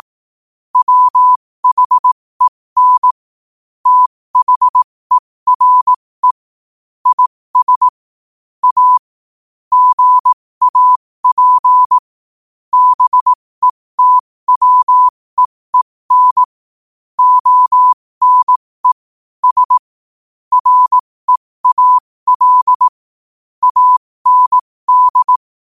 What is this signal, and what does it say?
Morse code, heard as a keyed tone, about 18 wpm, spaced as for 15 wpm, 1000 Hz. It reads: WHEN THERE IS A GAP BETWEEN ONE S REAL AND
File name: qotdfiles / 15wpm